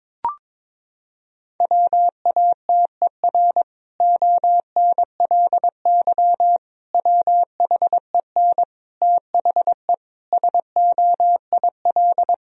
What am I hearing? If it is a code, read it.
Morse code, heard as a keyed tone, about 22 wpm, 700 Hz. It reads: WATER ONLY WHEN THE SOIL
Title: Sentences from Top 1000 Words - Review Part 4 - 22wpm